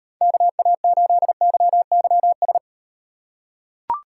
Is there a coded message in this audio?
Yes: Morse code from 0.2 to 2.6 s, heard as a keyed tone, about 38 wpm, 700 Hz. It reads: KA8YYS